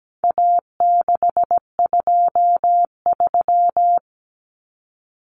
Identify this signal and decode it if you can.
Morse code, heard as a keyed tone, about 17 wpm, 700 Hz. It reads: A623
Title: Common Cut Numbers - 4-Digit 17wpm